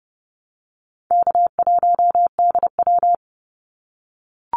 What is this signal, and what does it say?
Morse code, heard as a keyed tone, about 30 wpm, 700 Hz. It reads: K1DW
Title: Easy Call Signs - 30wpm